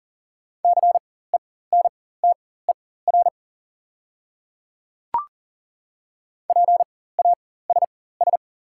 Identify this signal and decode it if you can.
Morse code, heard as a keyed tone, about 38 wpm, spaced as for 18 wpm, 690 Hz. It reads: CENTER PASS
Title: 4x Character Spacing - Top 500 Words - 40wpm